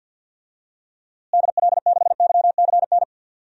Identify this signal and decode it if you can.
Morse code, heard as a keyed tone, about 52 wpm, 700 Hz. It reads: DL6YCN
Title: DXCC-100 Call Signs - 6 Characters - Repeated 3x - 50wpm